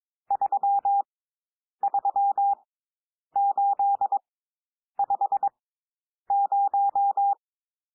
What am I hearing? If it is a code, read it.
Morse code, heard as a keyed tone, about 22 wpm, spaced as for 9 wpm, 800 Hz. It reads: 33850
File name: cwmp3 / N1 SOUND NUMBERS 22 wpm